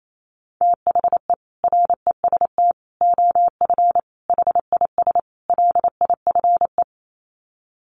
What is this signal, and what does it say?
Morse code, heard as a keyed tone, about 28 wpm, 700 Hz. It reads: THE REST OF HIS LIFE
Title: Sets of 5 Words 28wpm